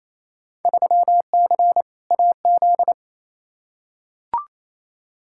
Morse code, heard as a keyed tone, about 28 wpm, 700 Hz. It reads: 3C AZ
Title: Field Day Exchanges Encore - 28wpm